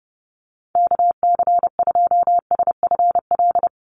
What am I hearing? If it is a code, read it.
Morse code, heard as a keyed tone, about 28 wpm, 700 Hz. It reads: KC2SFL
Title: USA Call Signs Repeated 2x 30wpm